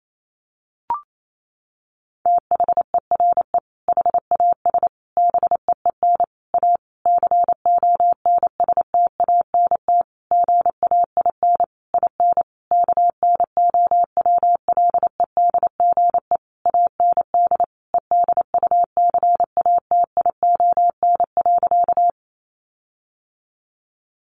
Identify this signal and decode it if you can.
Morse code, heard as a keyed tone, about 28 wpm, 700 Hz. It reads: THERE HAS BEEN A CONSTANT GAIN IN KNOWLEDGE AND EDUCATION.
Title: Sentences from Top 1500 Words 28wpm